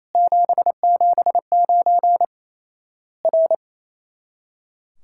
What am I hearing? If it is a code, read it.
Morse code, heard as a keyed tone, about 28 wpm, 700 Hz. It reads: 779 R
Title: Worked All Europe DX Contest - QTC Exchange - 28wpm